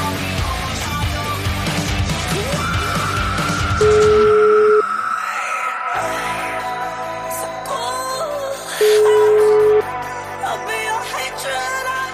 Ringback tone (Europe, 425 Hz), with noise.